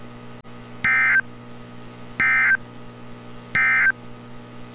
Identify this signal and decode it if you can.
Emergency Alert System (SAME) tones, heard as harsh buzzing bursts. It is heard from 0.7 to 4.0 s.
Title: NOAA Weather Radio - Altoona, PA